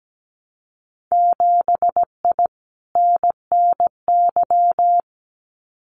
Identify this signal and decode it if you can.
Morse code, heard as a keyed tone, about 17 wpm, 700 Hz. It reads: 7I NNY
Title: Winter Field Day Exchanges Encore 17wpm